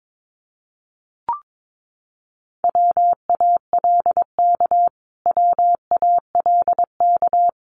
Morse code, heard as a keyed tone, about 22 wpm, 700 Hz. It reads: WALK WALK
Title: Top 300 Words Repeated 22wpm